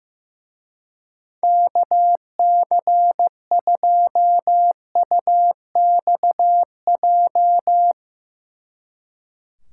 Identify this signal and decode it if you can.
Morse code, heard as a keyed tone, about 15 wpm, 700 Hz. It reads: KC2UXJ
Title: Speed Racing - USA Call Signs 15wpm